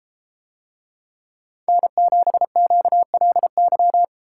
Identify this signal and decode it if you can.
Morse code, heard as a keyed tone, about 33 wpm, 700 Hz. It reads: N7QLY